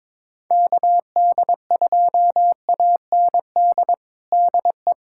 Morse code, heard as a keyed tone, about 22 wpm, 700 Hz. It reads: KD2AND DE
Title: USA Callsign DE Callsign - 6 Characters - 22wpm